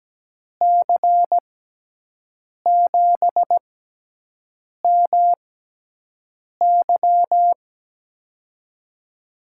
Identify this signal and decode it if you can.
Morse code, heard as a keyed tone, about 17 wpm, spaced as for 6 wpm, 700 Hz. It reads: C7MY
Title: USA Call Signs - 5 Characters - 6x Character Spacing - 17wpm